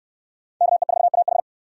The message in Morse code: KE4AH

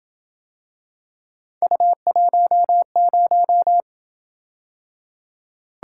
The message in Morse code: U10

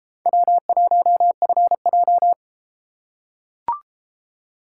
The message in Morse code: W1FJ

